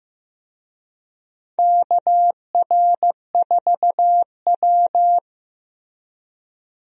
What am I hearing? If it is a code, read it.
Morse code, heard as a keyed tone, about 16 wpm, 700 Hz. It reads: KR4W
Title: USA Call Signs - 4 Characters - Repeated 3x - 15wpm